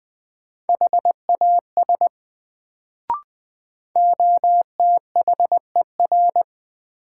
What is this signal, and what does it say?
Morse code, heard as a keyed tone, about 20 wpm, 700 Hz. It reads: HAS OTHER
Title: Sets of 2 Words - New Word Drill - 20wpm